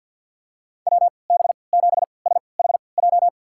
Morse code, heard as a keyed tone, about 48 wpm, spaced as for 28 wpm, 700 Hz. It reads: WB7SHJ
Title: USA Call Signs - 6 Characters - 3x Character Spacing - 50wpm